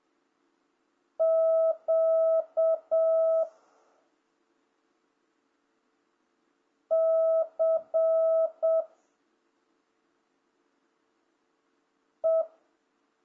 Morse code, heard as a keyed tone, about 7 wpm, 1300 Hz. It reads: Q C E